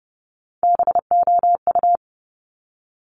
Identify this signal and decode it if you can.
Morse code, heard as a keyed tone, about 30 wpm, 700 Hz. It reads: BOU
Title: Top Three Letter Combinations in Words 30wpm